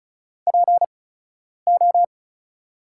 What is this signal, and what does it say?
Morse code, heard as a keyed tone, about 34 wpm, spaced as for 10 wpm, 690 Hz. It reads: PO
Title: 8x Character Spacing - Top 500 Words - 35wpm